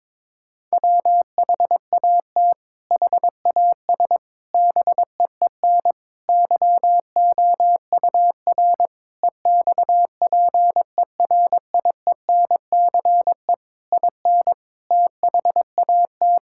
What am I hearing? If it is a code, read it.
Morse code, heard as a keyed tone, about 22 wpm, 700 Hz. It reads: WHAT HAS BEEN YOUR EXPERIENCE IN THAT
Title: Sentences from Top 1000 Words - Review Part 6 - 22wpm